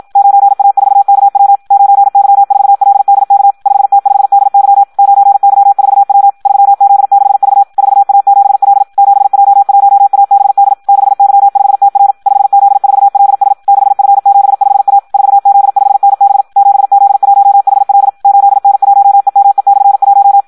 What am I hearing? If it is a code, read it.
Morse code, heard as a keyed tone, about 52 wpm, 800 Hz. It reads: 9A4KW 9Y4LDK 5T5DY 9Y4W 3Z6V 4A7L 7J1ABD 6Y5TR 5Z4LS 6F75A 4Z5NB 7Z1HL 8N1EME/1
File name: Audio practica CW / qrq-tx-54 2